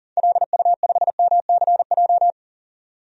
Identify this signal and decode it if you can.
Morse code, heard as a keyed tone, about 38 wpm, 700 Hz. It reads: LU5MCJ
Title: DXCC-100 Call Signs - 6 Characters - 40wpm